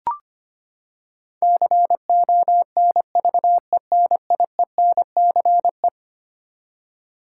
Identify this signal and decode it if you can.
Morse code, heard as a keyed tone, about 25 wpm, 700 Hz. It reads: CONVENIENCE
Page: morse-code-ninja-practice-25wpm